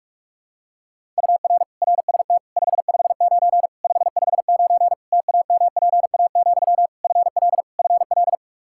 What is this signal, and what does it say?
Morse code, heard as a keyed tone, about 46 wpm, 700 Hz. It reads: UR RST 559 559 TAMPA, FL FL
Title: Practice QSO - Exchange 1 - Beginning - 45wpm